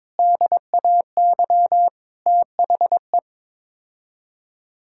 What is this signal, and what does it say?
Morse code, heard as a keyed tone, about 22 wpm, 700 Hz. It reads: DAY THE
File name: practice / 22wpm / Sets of 8 Words - 22wpm